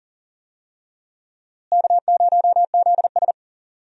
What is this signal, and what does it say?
Morse code, heard as a keyed tone, about 38 wpm, 700 Hz. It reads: K0ZS
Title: USA Call Signs - 4 Characters - Repeated 2x - 40wpm